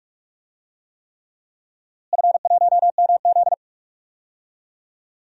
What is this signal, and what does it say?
Morse code, heard as a keyed tone, about 48 wpm, 700 Hz. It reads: F1MZ
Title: DXCC-100 Call Signs - 4 Characters - Repeated 3x - 45wpm